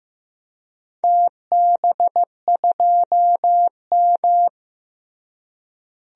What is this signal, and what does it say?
Morse code, heard as a keyed tone, about 15 wpm, 700 Hz. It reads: TB2M
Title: DXCC-100 Call Signs - 4 Characters - Repeated 2x - 15wpm